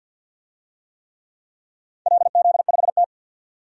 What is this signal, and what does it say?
Morse code, heard as a keyed tone, about 52 wpm, 700 Hz. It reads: LZ5T
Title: DXCC-100 Call Signs - 4 Characters - 50wpm